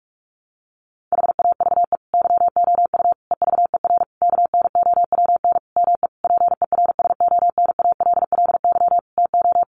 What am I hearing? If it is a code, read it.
Morse code, heard as a keyed tone, about 44 wpm, 700 Hz. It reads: HAVE YOU EVER KNOWN ME PERSONALLY TO
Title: Sentences from Top 1400 Words 45wpm